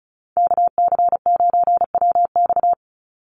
Morse code, heard as a keyed tone, about 34 wpm, 700 Hz. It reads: KC9WX
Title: USA Call Signs 35wpm